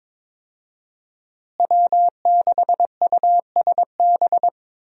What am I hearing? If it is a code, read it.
Morse code, heard as a keyed tone, about 22 wpm, 700 Hz. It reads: W6USB
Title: USA Call Signs - 5 Characters - Repeated 2x - 22wpm